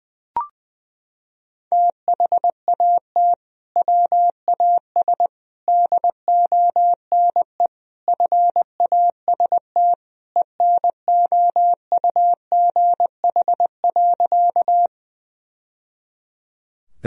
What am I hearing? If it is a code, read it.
Morse code, heard as a keyed tone, about 20 wpm, 700 Hz. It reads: THAT WAS DONE FAST ENOUGH.
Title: Sentences from Top 1000 Words - Review Part 2 - 20wpm